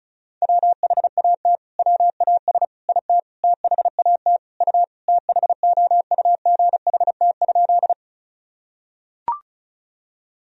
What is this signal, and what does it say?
Morse code, heard as a keyed tone, about 36 wpm, 700 Hz. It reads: WHAT WAS IT THAT U THOUGHT?
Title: Sentences from Top 300 Words and Abbreviations 35wpm